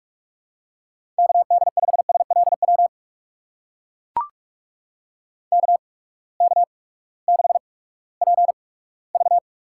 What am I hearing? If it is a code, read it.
Morse code, heard as a keyed tone, about 44 wpm, spaced as for 20 wpm, 700 Hz. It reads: KD5SLW KK6PV
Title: USA Call Signs - 6 Characters - 8x Character Spacing - 45wpm